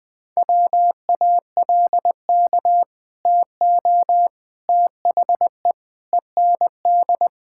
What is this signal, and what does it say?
Morse code, heard as a keyed tone, about 20 wpm, 700 Hz. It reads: WALK TO THE END
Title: Sentences with 4-Word Phrases - 20wpm